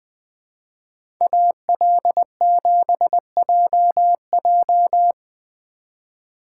Morse code, heard as a keyed tone, about 22 wpm, 700 Hz. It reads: AL7JJ
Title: Speed Racing - USA 2x2 Call Signs 20wpm